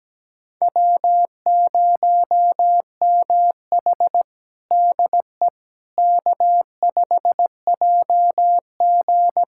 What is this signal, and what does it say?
Morse code, heard as a keyed tone, about 17 wpm, 700 Hz. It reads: W0MH DE K5JG
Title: USA Callsign DE Callsign - 4 Characters - 17wpm